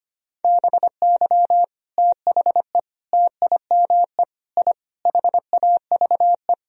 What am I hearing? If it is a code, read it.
Morse code, heard as a keyed tone, about 25 wpm, 700 Hz. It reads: BY THE TIME I HAVE
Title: Sentences from Top 1000 Words - Review Part 1 - 25wpm